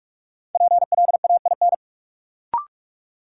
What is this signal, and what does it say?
Morse code, heard as a keyed tone, about 46 wpm, 700 Hz. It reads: PLAIN
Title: Sentences from Top 400 Words - New Word Drill - 45wpm